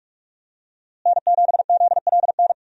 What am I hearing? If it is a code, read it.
Morse code, heard as a keyed tone, about 41 wpm, 700 Hz. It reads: N7ZLN
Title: USA Call Signs - 5 Characters - Repeated 3x - 45wpm